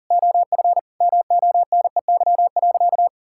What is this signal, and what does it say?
Morse code, heard as a keyed tone, about 37 wpm, 700 Hz. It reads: OF MONEY.
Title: Sentences from Top 1000 Words - Review Part 5 - 40wpm